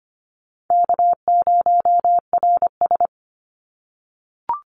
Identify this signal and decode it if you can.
Morse code, heard as a keyed tone, about 25 wpm, 700 Hz. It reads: K0RS
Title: NA SOTA Chaser Callsigns 25wpm